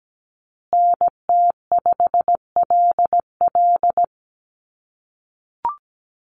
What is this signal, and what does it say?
Morse code, heard as a keyed tone, about 17 wpm, 700 Hz. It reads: NT5LL
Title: USA Call Signs Repeated 3x 17wpm